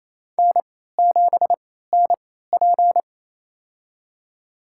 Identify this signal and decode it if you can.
Morse code, heard as a keyed tone, about 28 wpm, spaced as for 16 wpm, 700 Hz. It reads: N7NP
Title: USA Call Signs - 4 Characters - 3x Character Spacing - 28wpm